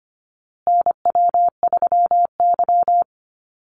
Morse code, heard as a keyed tone, about 25 wpm, 700 Hz. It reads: NW3Y
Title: Call Signs Repeated 2x - 25wpm